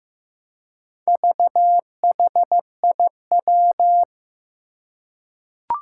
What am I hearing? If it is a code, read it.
Morse code, heard as a keyed tone, about 15 wpm, 700 Hz. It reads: VHIW
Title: Four Challenging Letter-Number Combinations 15wpm